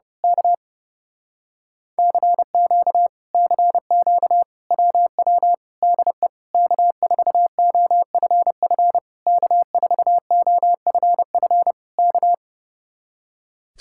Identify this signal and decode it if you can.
Morse code, heard as a keyed tone, about 28 wpm, 700 Hz. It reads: K CQ CQ WW DE K4OFF K4OFF K
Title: Speed-Racing Common QSO Phrases - 30wpm